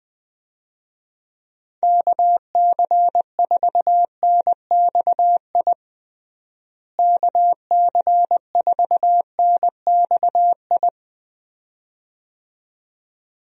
Morse code, heard as a keyed tone, about 20 wpm, 700 Hz. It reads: KC4NXI KC4NXI
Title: USA Call Signs - 6 Characters - Repeated 2x - 20wpm